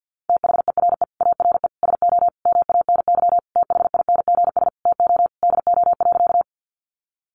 Morse code, heard as a keyed tone, about 50 wpm, 700 Hz. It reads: THERE ARE SO MANY THINGS TO DO.
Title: Additional Easy Phrases 50wpm